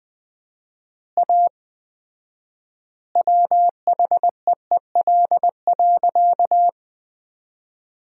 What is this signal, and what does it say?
Morse code, heard as a keyed tone, about 21 wpm, 700 Hz. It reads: A WHEEL.
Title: 4x Word-Spacing - Sentences from Top 500 Words - 20wpm